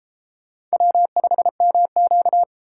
Morse code, heard as a keyed tone, about 31 wpm, 700 Hz. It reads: W5MQ